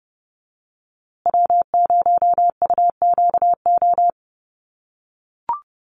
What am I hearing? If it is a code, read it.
Morse code, heard as a keyed tone, about 30 wpm, 700 Hz. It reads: W0UQO
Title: USA Call Signs Repeated 2x 30wpm